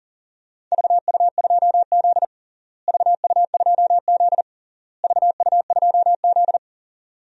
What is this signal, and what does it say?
Morse code, heard as a keyed tone, about 40 wpm, 700 Hz. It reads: VU2Z VU2Z VU2Z
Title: DXCC-100 Call Signs - 4 Characters - Repeated 3x - 40wpm